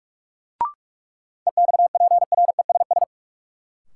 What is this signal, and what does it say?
Morse code, heard as a keyed tone, about 44 wpm, 700 Hz. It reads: EXPRESS
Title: Sentences from Top 1100 Words - New Word Drill - 45wpm